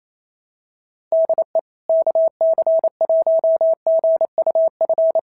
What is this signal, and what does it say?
Morse code, heard as a keyed tone, about 25 wpm, 650 Hz. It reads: DE KC1GUF
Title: SOTA QSO Practice 28wpm